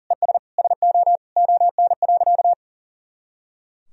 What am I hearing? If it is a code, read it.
Morse code, heard as a keyed tone, about 37 wpm, 700 Hz. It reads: ES SO ON.
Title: Sentences from Top 100 Words and Abbreviations 40wpm